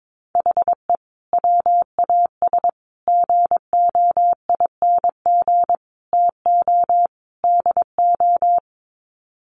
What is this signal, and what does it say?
Morse code, heard as a keyed tone, about 22 wpm, 700 Hz. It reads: HE WAS GOING TO DO
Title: Sets of 5 Words Encore2 22wpm